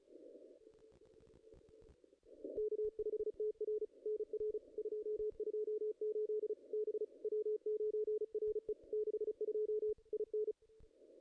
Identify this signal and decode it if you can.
Morse code, heard as a keyed tone, about 34 wpm, 440 Hz. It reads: K5TR NR 228 B W9RE 62 IN
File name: contests / ss-cw-k5tr-op-n5rz -2300